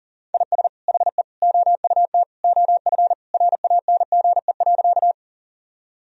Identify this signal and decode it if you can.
Morse code, heard as a keyed tone, about 40 wpm, 700 Hz. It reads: IS HE OUT OF RANGE.
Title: Sentences from Top 1000 Words - Review Part 4 - 40wpm